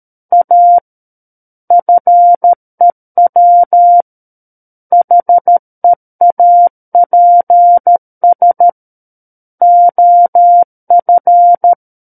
Morse code, heard as a keyed tone, about 13 wpm, 700 Hz. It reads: A FEW HEAPS OF
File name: Book One / 04-Chapter 4 - 13 WPM